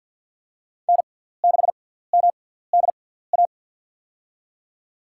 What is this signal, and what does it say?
Morse code, heard as a keyed tone, about 53 wpm, spaced as for 17 wpm, 700 Hz. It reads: N6MDA
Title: USA Call Signs - 6x Character Spacing - 50wpm